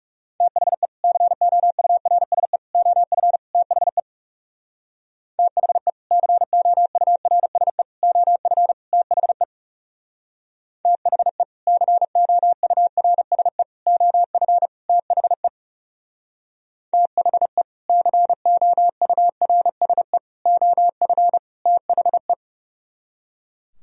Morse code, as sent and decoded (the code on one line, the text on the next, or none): - .... / -.-. --- ..- .-. ... / --- ..-. / - .... / - .... / -.-. --- ..- .-. ... / --- ..-. / - .... / - .... / -.-. --- ..- .-. ... / --- ..-. / - .... / - .... / -.-. --- ..- .-. ... / --- ..-. / - .... .
THE COURSE OF THE THE COURSE OF THE THE COURSE OF THE THE COURSE OF THE